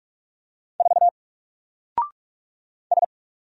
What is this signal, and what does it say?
Morse code, heard as a keyed tone, about 46 wpm, 700 Hz. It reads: 4 S
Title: Single Letter-Number 45wpm